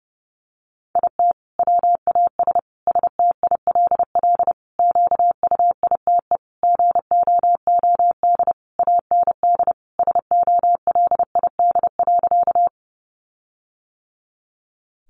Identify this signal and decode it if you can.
Morse code, heard as a keyed tone, about 30 wpm, 700 Hz. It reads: IT WAS STILL QUITE GOOD AND SOLID.
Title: Sentences from Top 1500 Words 30wpm